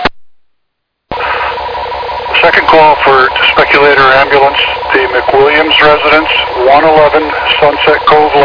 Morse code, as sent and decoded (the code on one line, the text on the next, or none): none